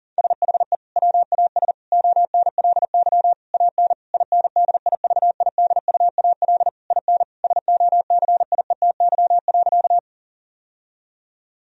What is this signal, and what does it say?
Morse code, heard as a keyed tone, about 40 wpm, 700 Hz. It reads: SHE WAS ONLY AN INDIVIDUAL IN SOCIETY.
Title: Sentences from Top 1000 Words - Review Part 4 - 40wpm